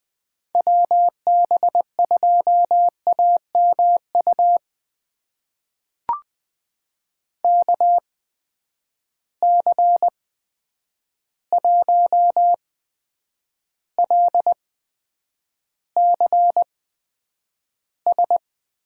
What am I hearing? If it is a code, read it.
Morse code, heard as a keyed tone, about 20 wpm, spaced as for 9 wpm, 700 Hz. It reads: WB2AMU KC1LCS
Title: USA Call Signs - 6 Characters - 8x Character Spacing - 20wpm